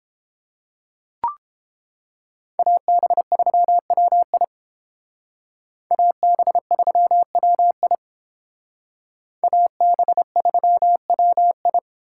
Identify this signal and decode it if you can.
Morse code, heard as a keyed tone, about 30 wpm, 700 Hz. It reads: AB3WI AB3WI AB3WI